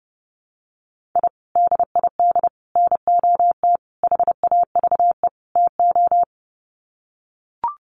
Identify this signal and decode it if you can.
Morse code, heard as a keyed tone, about 28 wpm, 700 Hz. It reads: I DID NOT HAVE TO
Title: Sets of 5 Words 30wpm